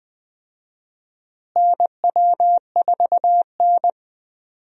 Morse code, heard as a keyed tone, about 20 wpm, 700 Hz. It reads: NW4N
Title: USA Call Signs - 4 Characters - Repeated 2x - 20wpm